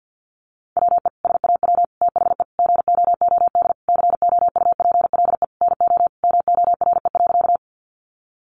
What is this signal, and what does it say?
Morse code, heard as a keyed tone, about 49 wpm, 700 Hz. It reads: WE SAW THE GOOD COUPLE NO MORE.
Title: Sentences from Top 1100 Words 50wpm